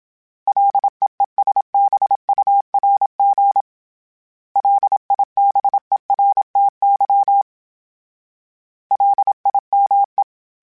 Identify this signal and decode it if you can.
Morse code, heard as a keyed tone, about 27 wpm, 800 Hz. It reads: LEESBURG LIBERTY LIME
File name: fromSMs / FromSMs mp3s / Cities@25